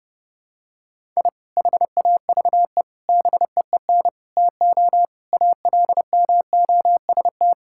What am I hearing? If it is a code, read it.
Morse code, heard as a keyed tone, about 30 wpm, 700 Hz. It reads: I HAVE BEEN TO ALMOST
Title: Sentences from Top 1000 Words - Review Part 5 - 30wpm